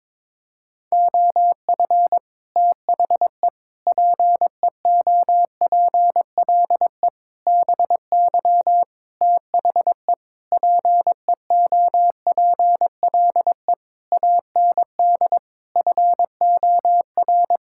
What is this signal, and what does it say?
Morse code, heard as a keyed tone, about 22 wpm, 700 Hz. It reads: OF THE PEOPLE BY THE PEOPLE AND FOR
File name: Sets of 9 Words - 22wpm